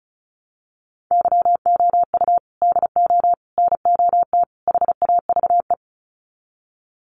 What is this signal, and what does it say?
Morse code, heard as a keyed tone, about 34 wpm, 700 Hz. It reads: YOU DO NOT HAVE
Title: Sets of 4 Words 35wpm